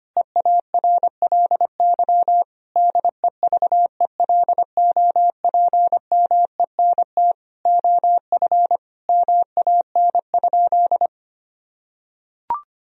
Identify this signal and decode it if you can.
Morse code, heard as a keyed tone, about 25 wpm, 700 Hz. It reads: EARLY DEVELOPMENT OF MAN?
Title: Sentences from Top 1000 Words - Review Part 4 - 25wpm